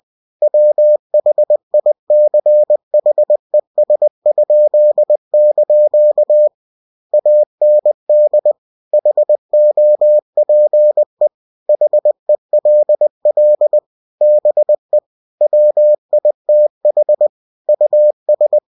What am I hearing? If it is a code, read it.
Morse code, heard as a keyed tone, about 20 wpm, 600 Hz. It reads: WHICHES?) AND HOPE HELL BE WITH US
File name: cw-books / winnie-the-pooh / winnie-20wpm-page-0104